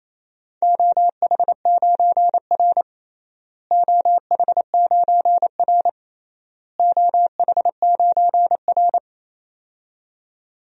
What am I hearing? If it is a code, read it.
Morse code, heard as a keyed tone, about 28 wpm, 700 Hz. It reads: OH9R OH9R OH9R
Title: DXCC-100 Call Signs - 4 Characters - Repeated 3x - 28wpm